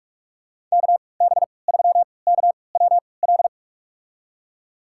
Morse code, heard as a keyed tone, about 46 wpm, spaced as for 25 wpm, 700 Hz. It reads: KB3KWL
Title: USA Call Signs - 6 Characters - 3x Character Spacing - 45wpm